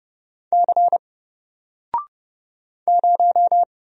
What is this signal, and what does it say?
Morse code, heard as a keyed tone, about 30 wpm, 700 Hz. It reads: C 0
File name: Single-Letter-Number-30wpm